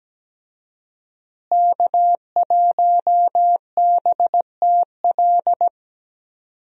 Morse code, heard as a keyed tone, about 17 wpm, 700 Hz. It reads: K1BTL
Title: USA Call Signs - 5 Characters - Repeated 2x - 17wpm